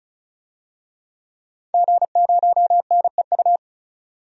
Morse code, heard as a keyed tone, about 36 wpm, 700 Hz. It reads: G0NEU
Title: DXCC-100 Call Signs - 5 Characters - 35wpm